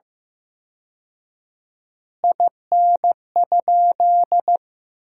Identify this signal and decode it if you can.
Morse code, heard as a keyed tone, about 15 wpm, 700 Hz. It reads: IN?
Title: 4x Word-Spacing - Sentences from Top 100 Words - 15wpm